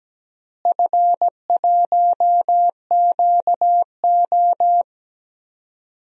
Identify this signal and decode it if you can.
Morse code, heard as a keyed tone, about 17 wpm, 700 Hz. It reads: F1QO
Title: DXCC-100 Call Signs - 4 Characters - Repeated 2x - 17wpm